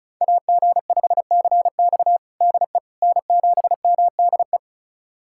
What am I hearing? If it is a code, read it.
Morse code, heard as a keyed tone, about 36 wpm, 690 Hz. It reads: AG5CX DE N7MDE
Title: USA Callsign DE Callsign - 5 Characters - 35wpm